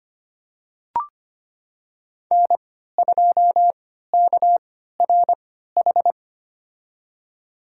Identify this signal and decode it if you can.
Morse code, heard as a keyed tone, about 25 wpm, spaced as for 14 wpm, 700 Hz. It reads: N2KRH
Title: USA Call Signs - 5 Characters - 3x Character Spacing - 25wpm